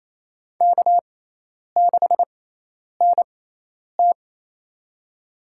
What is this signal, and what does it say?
Morse code, heard as a keyed tone, about 28 wpm, spaced as for 10 wpm, 700 Hz. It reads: K6NT
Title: USA Call Signs - 4 Characters - 6x Character Spacing - 28wpm